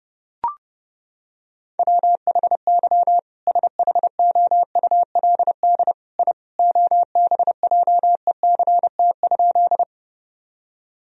Morse code, heard as a keyed tone, about 32 wpm, 700 Hz. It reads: WHY SHOULD I OBJECT?
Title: Sentences from Top 1000 Words - Review Part 3 - 30wpm